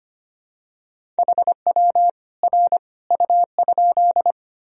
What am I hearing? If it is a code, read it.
Morse code, heard as a keyed tone, about 25 wpm, 700 Hz. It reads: HW R U?
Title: Sentences from Top 100 Words and Abbreviations 25wpm